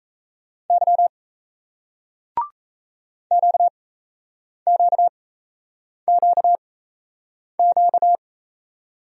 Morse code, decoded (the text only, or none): Y QQQQ